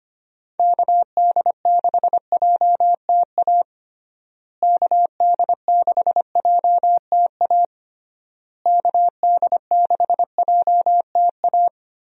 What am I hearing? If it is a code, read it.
Morse code, heard as a keyed tone, about 25 wpm, 700 Hz. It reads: KD6JTA KD6JTA KD6JTA
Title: USA Call Signs - 6 Characters - Repeated 3x - 25wpm